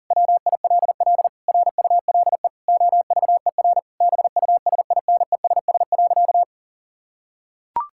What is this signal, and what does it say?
Morse code, heard as a keyed tone, about 40 wpm, 700 Hz. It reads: WILL RULE OVER BUSINESS.
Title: Sentences from Top 1000 Words - Review Part 5 - 40wpm